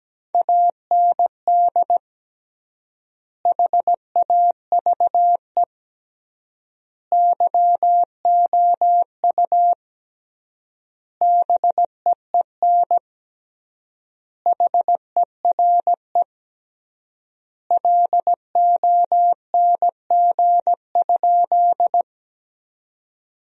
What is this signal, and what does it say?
Morse code, heard as a keyed tone, about 17 wpm, 700 Hz. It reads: AND HAVE YOU BEEN HERE LONG?
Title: 3x Word-Spacing - Sentences from Top 200 Words - 17wpm